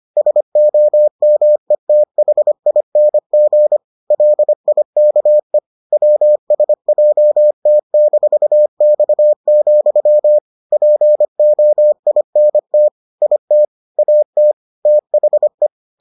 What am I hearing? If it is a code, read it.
Morse code, heard as a keyed tone, about 25 wpm, 600 Hz. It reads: SOMETHING LIKE WSJT-X, POINT IT AT THE